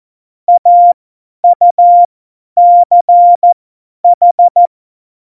Morse code, heard as a keyed tone, about 14 wpm, spaced as for 10 wpm, 700 Hz. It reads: AUCH
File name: Morsekurs / Witze / Witz 11. September 2018, 50 F70